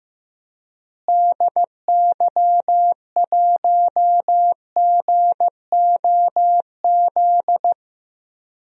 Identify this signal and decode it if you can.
Morse code, heard as a keyed tone, about 16 wpm, 700 Hz. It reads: DY1GOZ
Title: DXCC-100 Call Signs - 6 Characters - Repeated 2x - 15wpm